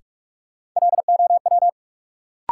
Morse code, heard as a keyed tone, about 47 wpm, 700 Hz. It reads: LOW
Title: Sentences from Top 500 Words - New Word Drill - 45wpm